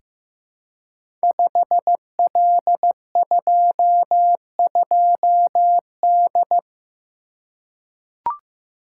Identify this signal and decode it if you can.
Morse code, heard as a keyed tone, about 15 wpm, 700 Hz. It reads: 5L22D
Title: Code Groups - 5 Letter:Numbers - 15wpm